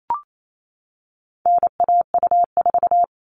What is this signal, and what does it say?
Morse code, heard as a keyed tone, about 28 wpm, 700 Hz. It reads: NAU4
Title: All Cut Numbers - 4-Digit 28wpm